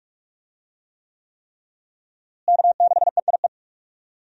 Morse code, heard as a keyed tone, about 45 wpm, 700 Hz. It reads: K6EIE
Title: USA Call Signs - 5 Characters - Repeated 3x - 45wpm